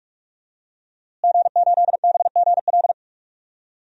Morse code, heard as a keyed tone, about 47 wpm, 700 Hz. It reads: G8BGL
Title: DXCC-100 Call Signs - 5 Characters - Repeated 3x - 45wpm